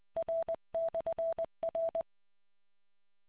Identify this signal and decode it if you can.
Morse code, heard as a keyed tone, about 20 wpm, 660 Hz. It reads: R/R